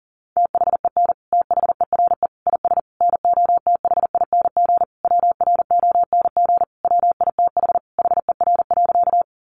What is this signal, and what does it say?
Morse code, heard as a keyed tone, about 40 wpm, 700 Hz. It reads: THEN THERE IS NOTHING WRONG WITH HER.
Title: Sentences from Top 900 Words 40wpm